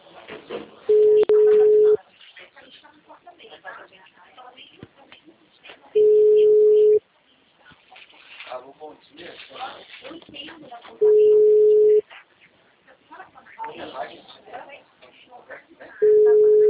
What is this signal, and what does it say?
Ringback tone (Europe, 425 Hz), with speech.
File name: out- -2019- - - .3386